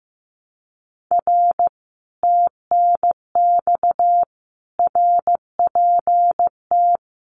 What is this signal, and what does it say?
Morse code, heard as a keyed tone, about 15 wpm, 700 Hz. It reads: R TNX RPT
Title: Common QSO Phrases Encore 15wpm